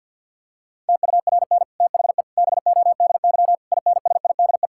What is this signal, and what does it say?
Morse code, heard as a keyed tone, about 50 wpm, 700 Hz. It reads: TURN THE BODY INSIDE
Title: Sentences from Top 1000 Words - Review Part 6 - 50wpm